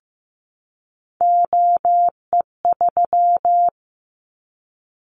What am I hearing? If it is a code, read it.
Morse code, heard as a keyed tone, about 15 wpm, 700 Hz. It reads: OE3
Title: Callsign Prefixes for Top 25 DXCC - 15wpm